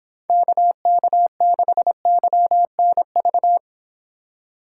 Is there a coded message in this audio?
Yes: Morse code from 0.3 to 3.6 s, heard as a keyed tone, about 26 wpm, 700 Hz. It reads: KK6YNV